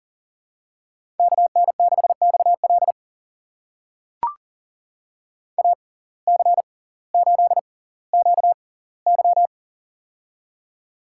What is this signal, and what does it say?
Morse code, heard as a keyed tone, about 39 wpm, spaced as for 20 wpm, 700 Hz. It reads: KN6XL AC8QY
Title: USA Call Signs - 5 Characters - 6x Character Spacing - 40wpm